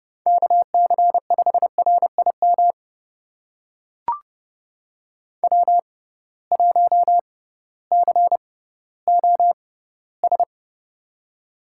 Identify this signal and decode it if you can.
Morse code, heard as a keyed tone, about 30 wpm, spaced as for 16 wpm, 700 Hz. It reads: KC5RIM W1COS
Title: USA Call Signs - 6x Character Spacing - 30wpm